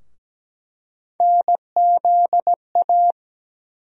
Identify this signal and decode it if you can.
Morse code, heard as a keyed tone, about 17 wpm, 700 Hz. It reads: NZA
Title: USA Call Sign Postfixes - 3 Letters 17wpm